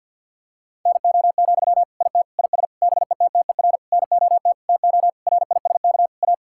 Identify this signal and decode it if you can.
Morse code, heard as a keyed tone, about 50 wpm, 700 Hz. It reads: NO, IT IS BETTER NOT TO RISK A
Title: Sentences from Top 1600 Words 50wpm